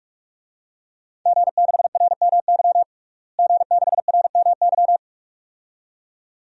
Morse code, heard as a keyed tone, about 46 wpm, 700 Hz. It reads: G6RMY G6RMY
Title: DXCC-100 Call Signs - 5 Characters - Repeated 2x - 45wpm